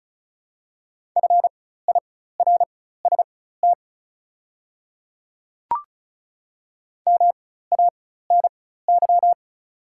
Morse code, heard as a keyed tone, about 36 wpm, spaced as for 16 wpm, 700 Hz. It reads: FIRST MANY
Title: 4x Character Spacing - Top 100 Words - 35wpm